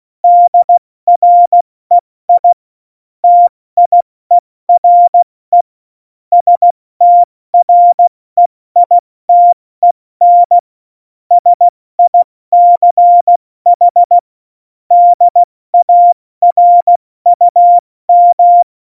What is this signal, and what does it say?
Morse code, heard as a keyed tone, about 16 wpm, spaced as for 14 wpm, 700 Hz. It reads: DREI TIERE STREITEN SICH DARUM
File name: Morsekurs / Witze / Witz 18. Dezember, 70 F80